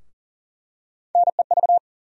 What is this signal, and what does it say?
Morse code, heard as a keyed tone, about 40 wpm, 700 Hz. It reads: NEV
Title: USA Call Sign Postfixes - 3 Letters 40wpm